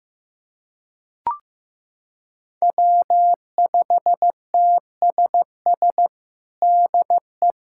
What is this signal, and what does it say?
Morse code, heard as a keyed tone, about 15 wpm, 700 Hz. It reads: W5TSS DE
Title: USA Callsign DE Callsign 15wpm